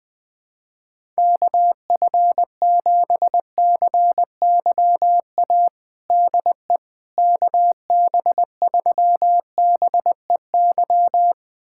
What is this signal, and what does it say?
Morse code, heard as a keyed tone, about 20 wpm, 700 Hz. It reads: KF7CYA DE KB3BEY
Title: USA Callsign DE Callsign - 6 Characters - 20wpm